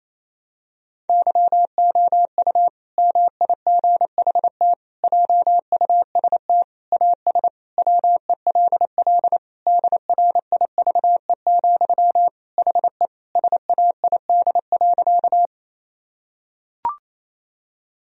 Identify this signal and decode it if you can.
Morse code, heard as a keyed tone, about 28 wpm, 700 Hz. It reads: YOU MIGHT JUST AS WELL DRIVE, HE SAID.
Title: Sentences from Top 1000 Words - Review Part 2 - 28wpm